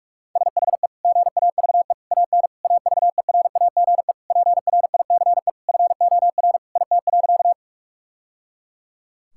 Morse code, heard as a keyed tone, about 46 wpm, 700 Hz. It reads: SHE GAVE AN AVERAGE PRICE FOR IT.
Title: Sentences from Top 1000 Words - Review Part 6 - 45wpm